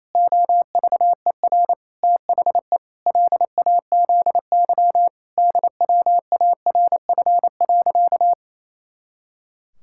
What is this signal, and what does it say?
Morse code, heard as a keyed tone, about 28 wpm, 700 Hz. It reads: OVER THE LAZY DWARF.
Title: Pangram Challenge v2 - 28wpm